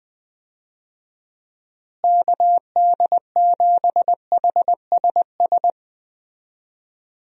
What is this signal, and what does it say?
Morse code, heard as a keyed tone, about 20 wpm, 700 Hz. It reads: KD7HSS
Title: USA Call Signs - 6 Characters - Repeated 2x - 20wpm